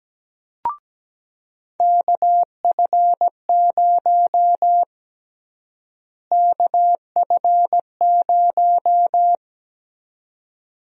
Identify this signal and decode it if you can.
Morse code, heard as a keyed tone, about 17 wpm, 700 Hz. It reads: KF0 KF0
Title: USA Call Sign Prefixes - Repeated 2x - 17wpm